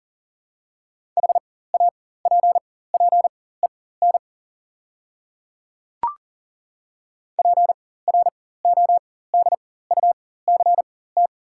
Morse code, heard as a keyed tone, about 40 wpm, spaced as for 19 wpm, 700 Hz. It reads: HAPPEN PRODUCT
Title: 4x Character Spacing - Top 500 Words - 40wpm